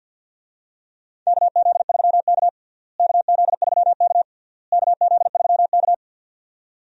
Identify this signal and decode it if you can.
Morse code, heard as a keyed tone, about 53 wpm, 700 Hz. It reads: KZ3K KZ3K KZ3K
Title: USA Call Signs - 4 Characters - Repeated 3x - 50wpm